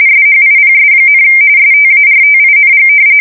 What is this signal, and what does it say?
RTTY, about 45.45 baud. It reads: SHIPS, STARTING
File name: fsk170 / usnavy1